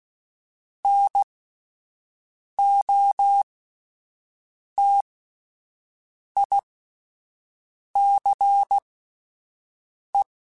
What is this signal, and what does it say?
Morse code, heard as a keyed tone, about 16 wpm, spaced as for 5 wpm, 800 Hz. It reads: NOTICE
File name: (spaced) / QSOs Spaced-mp3s / WD0BGP 1 @10 sp